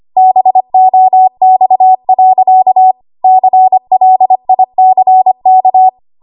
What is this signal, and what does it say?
Morse code, heard as a keyed tone, about 25 wpm, 750 Hz. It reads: BOX. CLICK
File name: Archive / 25 WPM / 25WPM